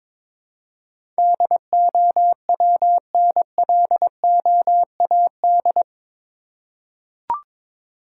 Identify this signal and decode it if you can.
Morse code, heard as a keyed tone, about 22 wpm, 700 Hz. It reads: DOWNLOAD
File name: practice / 22wpm / Eight Letter Words - 22wpm